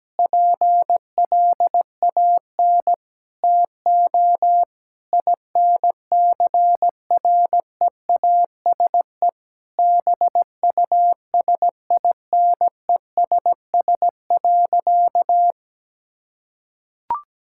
Morse code, heard as a keyed tone, about 17 wpm, 700 Hz. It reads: PLAN TO INCREASE BUSINESS.
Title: Sentences from Top 1000 Words - Review Part 6 - 17wpm